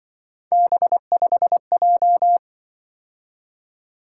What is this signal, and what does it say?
Morse code, heard as a keyed tone, about 24 wpm, 700 Hz. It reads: B5J
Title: Speed-Racing - Three Letter-Number Combinations - 20wpm